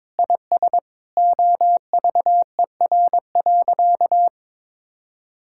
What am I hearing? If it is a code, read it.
Morse code, heard as a keyed tone, about 22 wpm, 700 Hz. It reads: IS OVER.
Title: Sentences from Top 1000 Words - Review Part 6 - 22wpm